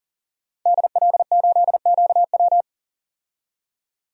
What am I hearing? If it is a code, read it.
Morse code, heard as a keyed tone, about 40 wpm, 700 Hz. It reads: DL8QW
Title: DXCC-100 Call Signs - 5 Characters - 40wpm